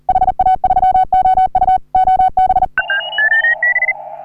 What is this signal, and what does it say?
Morse code, heard as a keyed tone, about 40 wpm, 750 Hz. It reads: HA3OU OB
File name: CW / HA3OU RHOBCW REC